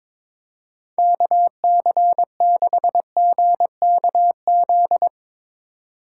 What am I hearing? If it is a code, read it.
Morse code, heard as a keyed tone, about 22 wpm, 700 Hz. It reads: KC6GKZ
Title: USA Call Signs - 4x Character Spacing - 22wpm